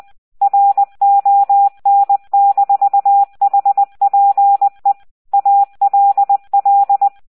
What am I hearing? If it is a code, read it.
Morse code, heard as a keyed tone, about 20 wpm, 800 Hz, with radio noise. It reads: RON-HPE ALL